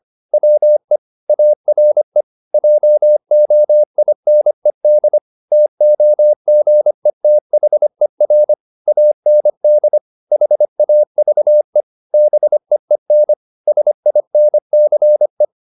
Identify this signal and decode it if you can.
Morse code, heard as a keyed tone, about 25 wpm, 600 Hz. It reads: WE ARE JOINED TOGETHER AND HAVE BEEN SINCE
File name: the old man and the sea part 1 25wpm